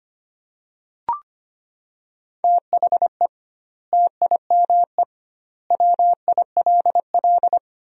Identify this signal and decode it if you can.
Morse code, heard as a keyed tone, about 25 wpm, 700 Hz. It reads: THE TIME WILL
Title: 2x Word-Spacing - Sentences from Top 100 Words - 25wpm